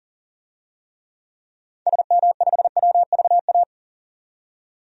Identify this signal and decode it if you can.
Morse code, heard as a keyed tone, about 38 wpm, 700 Hz. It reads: SM5WVA
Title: DXCC-100 Call Signs - 6 Characters - 40wpm